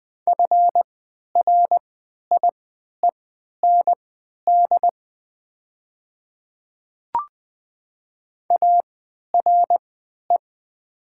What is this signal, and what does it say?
Morse code, heard as a keyed tone, about 20 wpm, spaced as for 11 wpm, 700 Hz. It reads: FRIEND ARE
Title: 3x Character Spacing - Top 300 Words - 20wpm